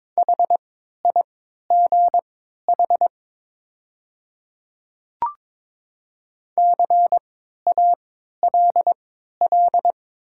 Morse code, heard as a keyed tone, about 22 wpm, spaced as for 13 wpm, 700 Hz. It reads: HIGH CALL
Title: 3x Character Spacing - Top 200 Words - 22wpm